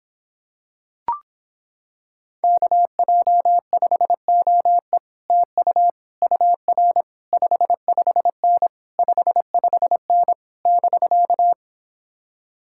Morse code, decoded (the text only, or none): KJ5OE TU UR 55N 55N <BK>